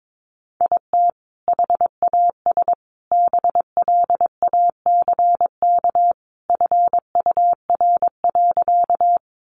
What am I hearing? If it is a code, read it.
Morse code, heard as a keyed tone, about 22 wpm, 700 Hz. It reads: IT HAS BLACK FUR.
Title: Easy Phrases 22wpm